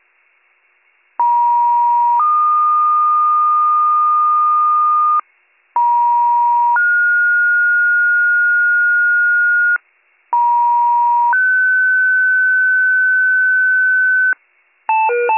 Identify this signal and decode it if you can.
Two-tone page, heard as a one-second tone, then a three-second tone, about 950 then 1220 Hz, about 950 then 1430 Hz, about 950 then 1500 Hz, with noise. It is heard from 1.2 to 14.3 s.